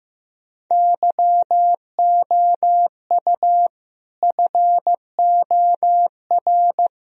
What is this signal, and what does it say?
Morse code, heard as a keyed tone, about 15 wpm, 700 Hz. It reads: YOU FOR
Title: Speed-Racing - Sets of 5 Words 15wpm